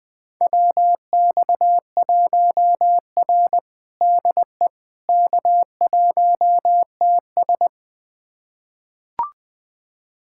Morse code, heard as a keyed tone, about 21 wpm, 700 Hz. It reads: WX1R DE K1TS
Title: USA Callsign DE Callsign - 4 Characters - 20wpm